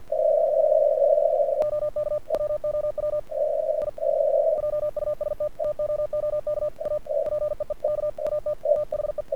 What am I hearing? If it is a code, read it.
Morse code, heard as a keyed tone, about 48 wpm, 1200 Hz, with radio noise. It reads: OK WOW I JUST TOOK A PEEK AT THE
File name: archived / CRSnet